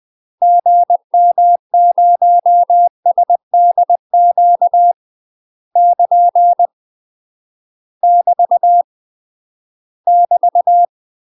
Morse code, heard as a keyed tone, about 20 wpm, 700 Hz. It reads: GM0SDQ <KN> <BT> <BT>
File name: UK MASTER QSOs TRAINING / 20WPM / UK MASTER QSOs 0023